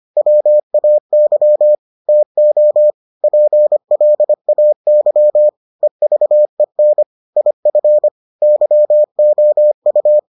Morse code, heard as a keyed tone, about 25 wpm, 600 Hz. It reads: WAY TO PLAY EVEN IF YOU